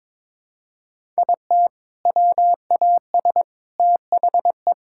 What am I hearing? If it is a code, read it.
Morse code, heard as a keyed tone, about 21 wpm, 700 Hz. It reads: IT WAS THE
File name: practice / 22wpm / Sentences with 3-Word Phrases - 22wpm